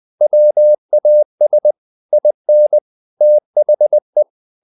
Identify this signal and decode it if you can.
Morse code, heard as a keyed tone, about 20 wpm, 600 Hz. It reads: WAS IN THE
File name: the old man and the sea part 2 20wpm